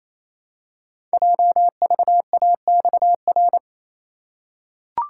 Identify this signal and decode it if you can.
Morse code, heard as a keyed tone, about 29 wpm, 700 Hz. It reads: JVAXR
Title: Code Groups - 5 Letter:Numbers - 28wpm